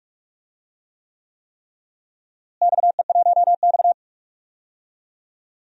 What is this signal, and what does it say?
Morse code, heard as a keyed tone, about 46 wpm, 700 Hz. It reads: XE1X
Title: DXCC-100 Call Signs - 4 Characters - Repeated 3x - 45wpm